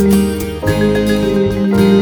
There is no code.